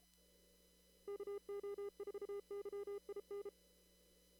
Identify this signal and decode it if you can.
Morse code, heard as a keyed tone, about 33 wpm, 400 Hz. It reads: KO4YIN
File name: CW / kt5j wpx cw-op-k5pi